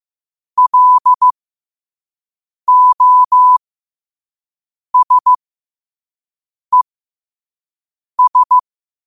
Morse code, heard as a keyed tone, about 15 wpm, spaced as for 5 wpm, 1000 Hz. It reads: LOSES
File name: qotdfiles / 5wpm